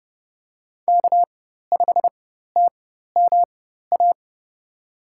Morse code, heard as a keyed tone, about 30 wpm, spaced as for 14 wpm, 700 Hz. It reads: K5TMA
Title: USA Call Signs - 5 Characters - 4x Character Spacing - 30wpm